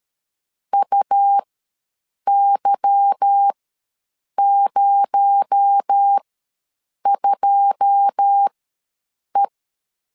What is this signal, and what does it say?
Morse code, heard as a keyed tone, about 13 wpm, spaced as for 7 wpm, 800 Hz. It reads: UY02E
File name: LEZIONI / 9 28 ws 12wpm 3354